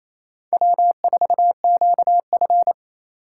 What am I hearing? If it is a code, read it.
Morse code, heard as a keyed tone, about 36 wpm, 700 Hz. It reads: W4QF